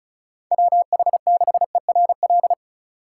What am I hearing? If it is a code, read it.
Morse code, heard as a keyed tone, about 43 wpm, 700 Hz. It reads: WH6ERL
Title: Speed Racing - USA 2x3 Call Signs 35wpm